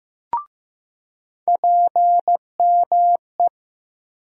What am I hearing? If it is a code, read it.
Morse code, heard as a keyed tone, about 15 wpm, 700 Hz. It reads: PME
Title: USA Call Sign Postfixes - 3 Letters 15wpm